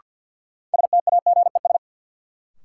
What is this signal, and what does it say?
Morse code, heard as a keyed tone, about 50 wpm, 700 Hz. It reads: STAGES
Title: Sets of 5 Words Encore - New Word Drill - 50wpm